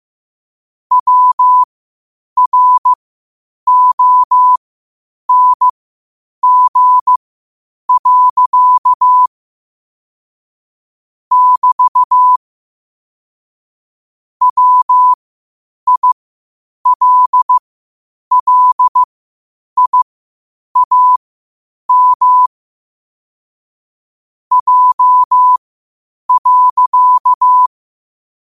Morse code, heard as a keyed tone, about 15 wpm, spaced as for 8 wpm, 1000 Hz. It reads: WRONG. <BT> WILLIAM J.
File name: qotdfiles / 8wpm